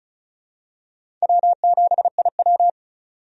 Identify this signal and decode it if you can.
Morse code, heard as a keyed tone, about 34 wpm, 700 Hz. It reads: W7IW
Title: USA Call Signs - 4 Characters - 35wpm